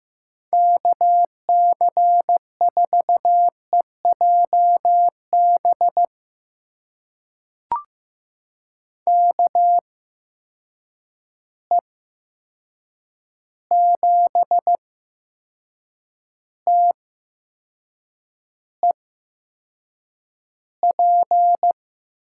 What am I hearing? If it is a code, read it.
Morse code, heard as a keyed tone, about 15 wpm, spaced as for 6 wpm, 700 Hz. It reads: KC4EJB KE7TEP
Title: USA Call Signs - 8x Character Spacing - 15wpm